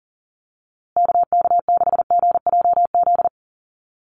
Morse code, heard as a keyed tone, about 40 wpm, 700 Hz. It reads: KK6GJZ
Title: USA Call Signs Repeated 3x 40wpm